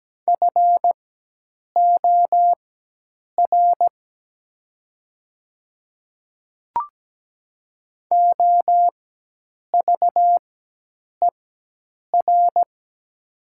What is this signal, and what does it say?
Morse code, heard as a keyed tone, about 17 wpm, spaced as for 8 wpm, 700 Hz. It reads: FOR OVER
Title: 4x Character Spacing - Top 100 Words - 17wpm